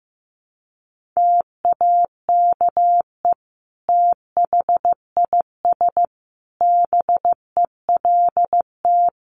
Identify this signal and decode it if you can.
Morse code, heard as a keyed tone, about 15 wpm, 700 Hz. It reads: TAKE THIS BELT
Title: Sentences from Top 1400 Words 15wpm